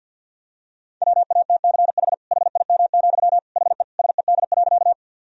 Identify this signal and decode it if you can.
Morse code, heard as a keyed tone, about 49 wpm, 700 Hz. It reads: WATCH HIM, HE SED.
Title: Sentences from Top 300 Words and Abbreviations 50wpm